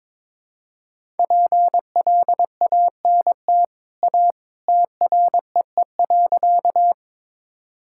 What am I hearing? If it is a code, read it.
Morse code, heard as a keyed tone, about 21 wpm, 700 Hz. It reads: PLANT A TREE.
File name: practice / 22wpm / Easy Sentences with Hints - 22wpm